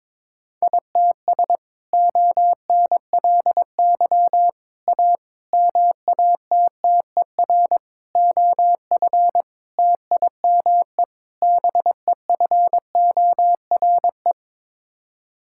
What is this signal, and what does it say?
Morse code, heard as a keyed tone, about 22 wpm, 700 Hz. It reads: ITS ONLY A MATTER OF TIME BEFORE
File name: practice / 22wpm / Sets of 7 Words - 22wpm